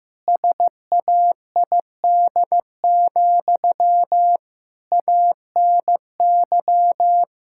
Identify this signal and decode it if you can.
Morse code, heard as a keyed tone, about 15 wpm, 700 Hz. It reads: SAID, ANY
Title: Sentences from Top 1000 Words - Review Part 6 - 15wpm